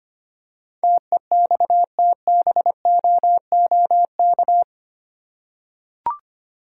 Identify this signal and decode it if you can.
Morse code, heard as a keyed tone, about 25 wpm, 700 Hz. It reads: TEXTBOOK
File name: 8 Letter Words - First 3 Letters Given - 25wpm